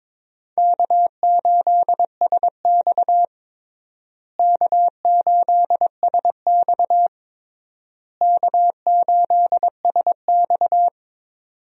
Morse code, heard as a keyed tone, about 22 wpm, 700 Hz. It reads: K8SX K8SX K8SX
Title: USA Call Signs - 4 Characters - Repeated 3x - 22wpm